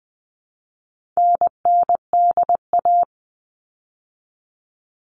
Morse code, heard as a keyed tone, about 20 wpm, 700 Hz. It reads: NNDA
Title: All Cut Numbers - 4-Digit 20wpm